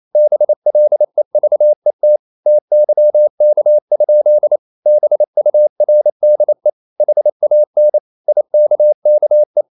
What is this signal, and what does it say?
Morse code, heard as a keyed tone, about 28 wpm, 600 Hz. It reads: BLEVET TYK? BURDE HAN IKKE
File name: Ferie QRQ 28wpm